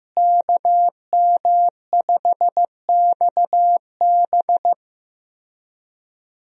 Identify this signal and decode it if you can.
Morse code, heard as a keyed tone, about 15 wpm, 700 Hz. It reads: KM5XB
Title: USA Call Signs - 5 Characters - Repeated 3x - 15wpm